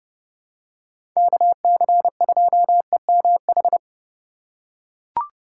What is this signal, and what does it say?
Morse code, heard as a keyed tone, about 30 wpm, 700 Hz. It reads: KC2EMH
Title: USA Call Signs - 6 Characters - 2x Character Spacing - 30wpm